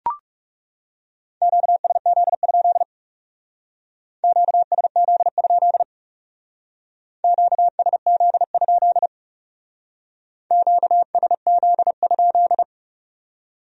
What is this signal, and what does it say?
Morse code, heard as a keyed tone, about 40 wpm, 700 Hz. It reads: QSZ? QSZ? QSZ? QSZ?